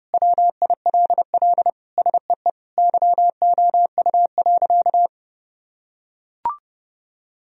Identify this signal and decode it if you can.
Morse code, heard as a keyed tone, about 32 wpm, 700 Hz. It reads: WILL SEE YOU.
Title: Sentences from Top 1000 Words - Review Part 1 - 30wpm